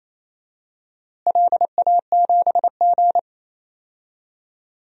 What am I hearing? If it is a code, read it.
Morse code, heard as a keyed tone, about 27 wpm, 700 Hz. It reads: LA7G
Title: DXCC-100 Call Signs - 4 Characters - Repeated 3x - 28wpm